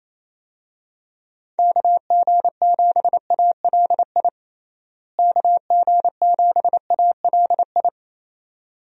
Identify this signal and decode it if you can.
Morse code, heard as a keyed tone, about 28 wpm, 700 Hz. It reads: KG7ALI KG7ALI
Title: USA Call Signs - 6 Characters - Repeated 2x - 28wpm